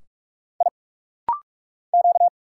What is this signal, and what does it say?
Morse code, heard as a keyed tone, about 44 wpm, 700 Hz. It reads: I Q
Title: Single Letter-Number Alt - Rapid-Fire 45wpm